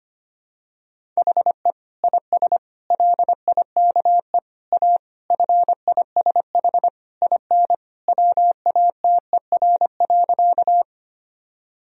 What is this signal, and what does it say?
Morse code, heard as a keyed tone, about 25 wpm, 700 Hz. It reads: HE IS LIKE A FISH IN WATER.
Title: Sentences from Top 1000 Words - Review Part 1 - 25wpm